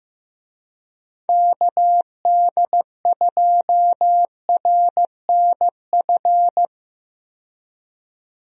Morse code, heard as a keyed tone, about 15 wpm, 700 Hz. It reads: KD2RNF
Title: USA Call Signs - 6 Characters - 15wpm